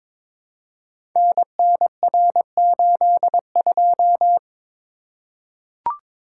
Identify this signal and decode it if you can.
Morse code, heard as a keyed tone, about 22 wpm, 700 Hz. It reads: NNR82
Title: Code Groups - 5 Letter:Numbers - 22wpm